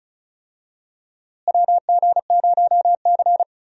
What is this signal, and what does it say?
Morse code, heard as a keyed tone, about 36 wpm, 700 Hz. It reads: WG0C
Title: USA Call Signs - 4 Characters - Repeated 2x - 35wpm